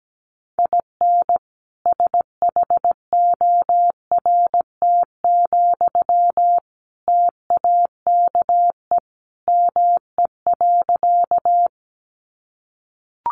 Morse code, heard as a keyed tone, about 17 wpm, 700 Hz. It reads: IN SHORT, TAKE ME.
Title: Sentences from Top 400 Words 17wpm